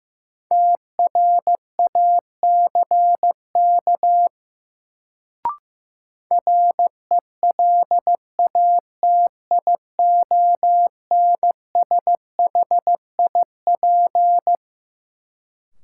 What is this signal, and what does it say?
Morse code, heard as a keyed tone, about 15 wpm, 700 Hz. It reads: TRACK RELATIONSHIP
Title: Sentences from Top 1100 Words - New Word Drill - 15wpm